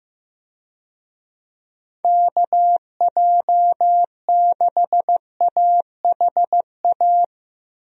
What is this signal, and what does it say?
Morse code, heard as a keyed tone, about 15 wpm, 700 Hz. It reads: KJ6AHA
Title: USA Call Signs - 6 Characters - 15wpm